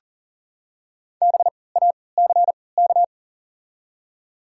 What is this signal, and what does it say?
Morse code, heard as a keyed tone, about 40 wpm, spaced as for 23 wpm, 700 Hz. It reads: BACK
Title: 3x Character Spacing - Top 200 Words - 40wpm